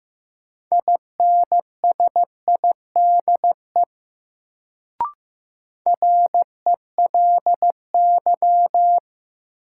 Morse code, heard as a keyed tone, about 15 wpm, 700 Hz. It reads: INSIDE RELY
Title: Sentences from Top 1200 Words - New Word Drill Alt - 15wpm